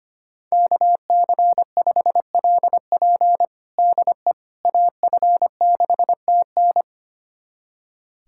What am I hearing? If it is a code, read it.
Morse code, heard as a keyed tone, about 25 wpm, 700 Hz. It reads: KC5LP DE AF6TN
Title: USA Callsign DE Callsign - 5 Characters - 25wpm